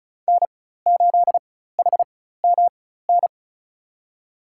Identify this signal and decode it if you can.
Morse code, heard as a keyed tone, about 36 wpm, spaced as for 16 wpm, 700 Hz. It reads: N8HMN
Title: USA Call Signs - 5 Characters - 4x Character Spacing - 35wpm